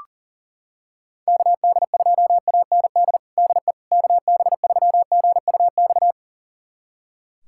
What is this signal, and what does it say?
Morse code, heard as a keyed tone, about 37 wpm, 700 Hz. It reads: KD2AND DE KB3GUX
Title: USA Callsign DE Callsign - 6 Characters - 40wpm